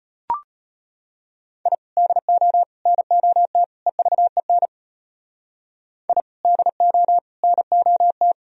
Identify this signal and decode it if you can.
Morse code, heard as a keyed tone, about 32 wpm, 700 Hz. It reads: I DO NOT EVEN I DO NOT